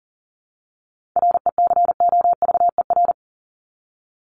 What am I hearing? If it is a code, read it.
Morse code, heard as a keyed tone, about 40 wpm, 700 Hz. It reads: RECOVER
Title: Top 1751-2000 Words 40wpm